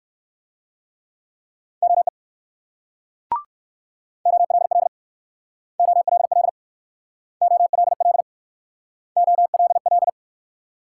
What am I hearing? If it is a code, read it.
Morse code, heard as a keyed tone, about 58 wpm, 700 Hz. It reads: OE OLL OLL OLL OLL